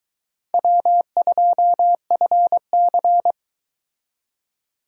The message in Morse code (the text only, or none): W2FC